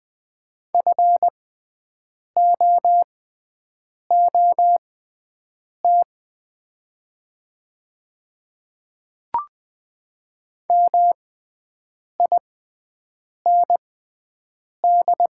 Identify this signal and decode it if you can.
Morse code, heard as a keyed tone, about 20 wpm, spaced as for 7 wpm, 700 Hz. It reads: FOOT MIND
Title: 6x Character Spacing - Top 500 Words - 20wpm